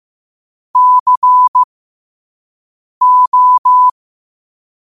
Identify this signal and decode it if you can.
Morse code, heard as a keyed tone, about 15 wpm, spaced as for 5 wpm, 1000 Hz. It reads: CO
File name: qotdfiles / 5wpm